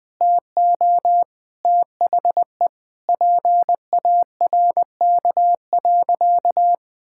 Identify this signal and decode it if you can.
Morse code, heard as a keyed tone, about 21 wpm, 700 Hz. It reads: TO THE PARK.
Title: Sentences with 4-Word Phrases - 20wpm